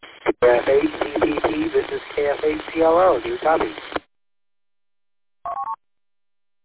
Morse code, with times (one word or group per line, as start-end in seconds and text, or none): none